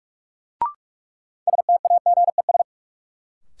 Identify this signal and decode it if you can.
Morse code, heard as a keyed tone, about 44 wpm, 700 Hz. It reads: STAGES
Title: Sets of 5 Words Encore - New Word Drill - 45wpm